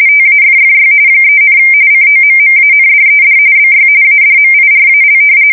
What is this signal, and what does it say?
RTTY, about 45.45 baud. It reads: TAFFY 3, COMMANDED BY ADMIRAL
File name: fsk170 / battleshipnj